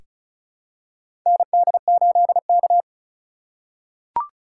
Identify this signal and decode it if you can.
Morse code, heard as a keyed tone, about 34 wpm, 700 Hz. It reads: ND8K
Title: USA Call Signs - 4 Characters - 4x Character Spacing - 35wpm